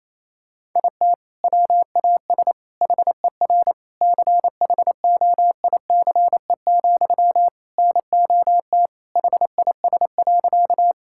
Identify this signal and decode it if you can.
Morse code, heard as a keyed tone, about 28 wpm, 700 Hz. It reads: IT WAS HER CHOICE, NOT HIS.
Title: Sentences from Top 1000 Words - Review Part 6 - 28wpm